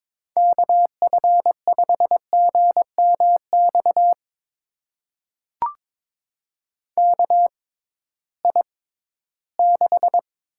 Morse code, heard as a keyed tone, about 22 wpm, spaced as for 14 wpm, 700 Hz. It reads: KF5GMX KI6
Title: USA Call Signs - 6x Character Spacing - 22wpm